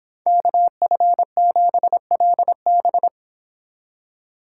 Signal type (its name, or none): Morse code